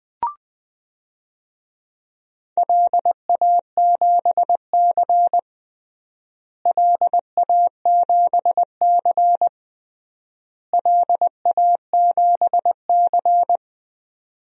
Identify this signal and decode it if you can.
Morse code, heard as a keyed tone, about 20 wpm, 690 Hz. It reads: LA7C LA7C LA7C
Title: DXCC-100 Call Signs - 4 Characters - Repeated 3x - 20wpm